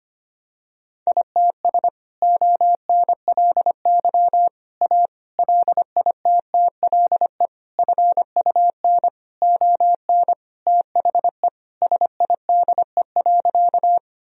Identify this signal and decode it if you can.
Morse code, heard as a keyed tone, about 25 wpm, 700 Hz. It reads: ITS ONLY A LITTLE FUN ON THE SIDE.
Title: Sentences from Top 1000 Words - Review Part 4 - 25wpm